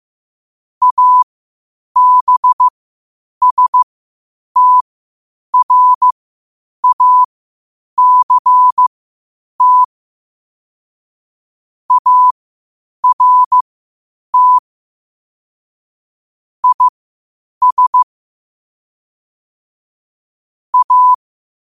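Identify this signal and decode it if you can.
Morse code, heard as a keyed tone, about 15 wpm, spaced as for 8 wpm, 1000 Hz. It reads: ABSTRACT ART IS A
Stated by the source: Morse code quotes at 8 WPM.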